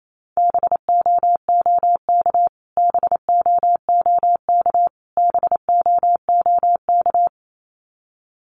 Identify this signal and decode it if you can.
Morse code, heard as a keyed tone, about 28 wpm, 700 Hz. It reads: BOOK BOOK BOOK
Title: Top 300 Words Repeated 28wpm